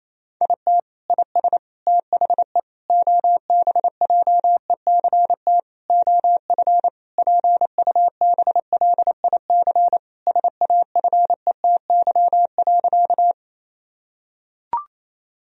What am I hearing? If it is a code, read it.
Morse code, heard as a keyed tone, about 28 wpm, 700 Hz. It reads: IT IS THE OBJECT OF PUBLIC SAFETY.
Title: Sentences from Top 1000 Words - Review Part 4 - 28wpm